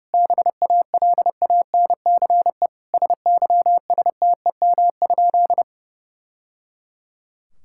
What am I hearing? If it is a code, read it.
Morse code, heard as a keyed tone, about 30 wpm, 700 Hz. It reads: BALANCE SYSTEM?
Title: Sentences from Top 1000 Words - Review Part 6 - 30wpm